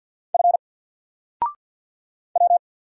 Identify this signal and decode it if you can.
Morse code, heard as a keyed tone, about 50 wpm, 700 Hz. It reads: F W
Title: Single Character (Letter:Number:Punctuation:Prosign) 50wpm